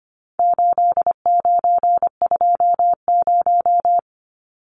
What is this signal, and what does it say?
Morse code, heard as a keyed tone, about 25 wpm, 700 Hz. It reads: 8920
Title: Numbers - 4-Digit 25wpm